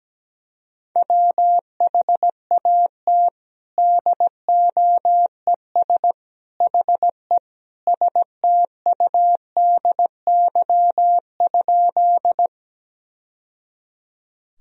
Morse code, heard as a keyed tone, about 17 wpm, 700 Hz. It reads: WHAT DOES HE STUDY?
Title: Sentences from Top 1000 Words - Review Part 1 - 17wpm